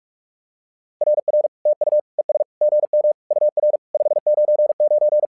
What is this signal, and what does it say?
Morse code, heard as a keyed tone, about 46 wpm, 600 Hz. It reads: RR TU ES GM UR 599
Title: SOTA QSO Practice 45wpm